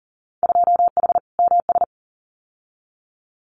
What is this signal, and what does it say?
Morse code, heard as a keyed tone, about 40 wpm, 700 Hz. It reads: 2H MS
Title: Winter Field Day Exchanges 40wpm